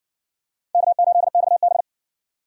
Morse code, heard as a keyed tone, about 60 wpm, 700 Hz. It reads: K8XB